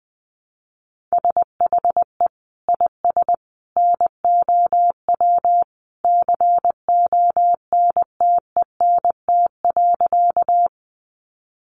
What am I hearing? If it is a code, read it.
Morse code, heard as a keyed tone, about 20 wpm, 700 Hz. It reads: SHE IS NOW CONTENT.
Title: Sentences from Top 900 Words 20wpm